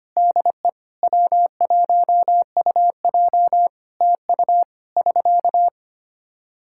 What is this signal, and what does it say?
Morse code, heard as a keyed tone, about 25 wpm, 700 Hz. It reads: DE W1UJ TU <SK>
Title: K1USN SST QSO Practice 25wpm